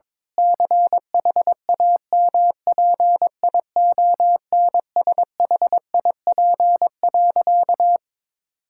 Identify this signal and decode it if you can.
Morse code, heard as a keyed tone, about 22 wpm, 700 Hz. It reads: CHAMPIONSHIP.
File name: practice / 22wpm / Sentences with 8-Word Phrases - 22wpm